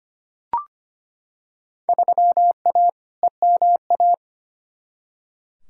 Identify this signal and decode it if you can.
Morse code, heard as a keyed tone, about 25 wpm, 700 Hz. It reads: 3A EMA
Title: Field Day Exchanges - 25wpm